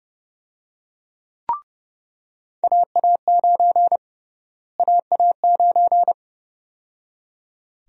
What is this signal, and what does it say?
Morse code, heard as a keyed tone, about 30 wpm, 700 Hz. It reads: AA9 AA9
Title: USA Call Sign Prefixes - Repeated 2x - 30wpm